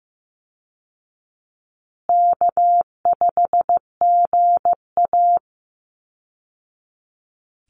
Morse code, heard as a keyed tone, about 15 wpm, 700 Hz. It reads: K5GA
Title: Easy Call Signs Repeated 3x - 15wpm